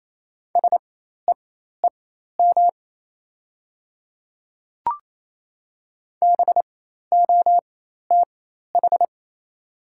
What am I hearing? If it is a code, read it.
Morse code, heard as a keyed tone, about 27 wpm, spaced as for 13 wpm, 700 Hz. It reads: SEEM BOTH
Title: 4x Character Spacing - Top 300 Words - 28wpm